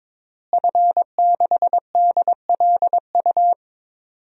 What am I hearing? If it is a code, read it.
Morse code, heard as a keyed tone, about 22 wpm, 700 Hz. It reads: F6DLU
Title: DXCC-100 Call Signs - 5 Characters - Repeated 2x - 22wpm